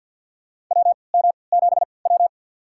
Morse code, heard as a keyed tone, about 52 wpm, spaced as for 29 wpm, 700 Hz. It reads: WM7W
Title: USA Call Signs - 4 Characters - 3x Character Spacing - 50wpm